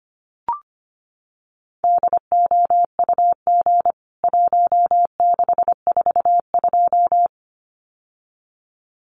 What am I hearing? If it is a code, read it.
Morse code, heard as a keyed tone, about 25 wpm, 700 Hz. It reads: DOUG 1642
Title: CWT Exchange 25wpm